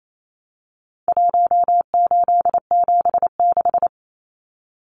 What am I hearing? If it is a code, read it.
Morse code, heard as a keyed tone, about 28 wpm, 700 Hz. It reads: 1876
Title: Numbers - Contest Style - 4-Digit 28wpm